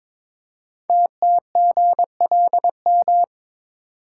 Morse code, heard as a keyed tone, about 23 wpm, 700 Hz. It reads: TTGLM
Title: Code Groups - 5 Letters - 22wpm